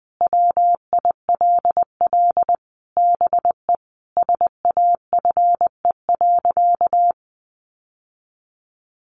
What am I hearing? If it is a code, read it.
Morse code, heard as a keyed tone, about 20 wpm, 700 Hz. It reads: WILL BE SAFE.
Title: Sentences from Top 1300 Words 20wpm